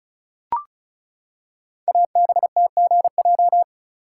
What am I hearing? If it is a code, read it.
Morse code, heard as a keyed tone, about 36 wpm, 700 Hz. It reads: ABTGJ
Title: Code Groups - 5 Letters - 35wpm